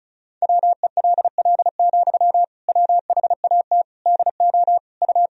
Morse code, heard as a keyed tone, about 36 wpm, 700 Hz. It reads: WELL, WHAT DO U
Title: Sentences from Top 200 Words and Abbreviations 35wpm